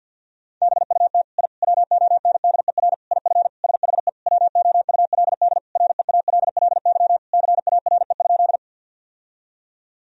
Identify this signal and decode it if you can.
Morse code, heard as a keyed tone, about 50 wpm, 700 Hz. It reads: BUT I WONDER IF SHE WOULD REALLY CARE?
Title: Sentences from Top 1000 Words - Review Part 3 - 50wpm